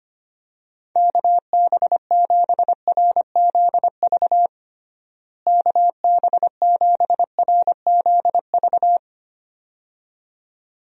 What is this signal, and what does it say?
Morse code, heard as a keyed tone, about 25 wpm, 700 Hz. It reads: KB7RZV KB7RZV
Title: USA Call Signs - 6 Characters - Repeated 2x - 25wpm